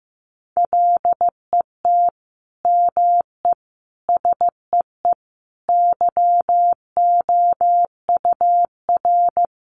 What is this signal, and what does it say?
Morse code, heard as a keyed tone, about 14 wpm, 700 Hz. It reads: LET ME SEE YOUR
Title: Sentences from Top 500 Words 15wpm